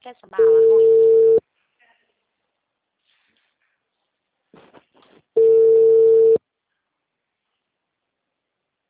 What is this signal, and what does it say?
Ringback tone (China, 450 Hz), with speech.
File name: out- -2009- - - .34695